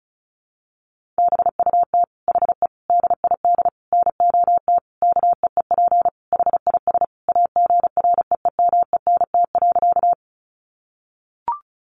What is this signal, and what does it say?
Morse code, heard as a keyed tone, about 34 wpm, 700 Hz. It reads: BUT HE DID NOT KEEP HIS AGREEMENT.
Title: Sentences from Top 1500 Words 35wpm